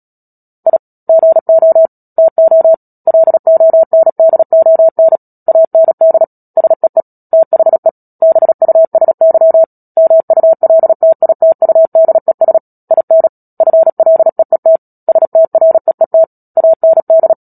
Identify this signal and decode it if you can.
Morse code, heard as a keyed tone, about 37 wpm, 650 Hz. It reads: I GO TO LONDON AND SEE THE BUSY MULTITUDES IN FLEET STREET AND
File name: Book Two / 10-Chapter 10 - Epilogue - 36 WPM